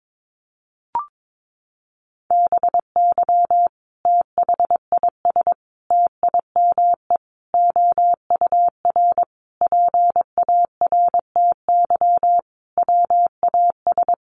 Morse code, heard as a keyed tone, about 22 wpm, 700 Hz. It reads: BY THIS TIME OUR PARTY WAS
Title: Sentences from Top 1400 Words 22wpm